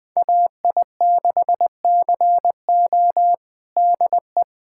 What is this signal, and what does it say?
Morse code, heard as a keyed tone, about 19 wpm, 700 Hz. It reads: AI6CO DE
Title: USA Callsign DE Callsign 20wpm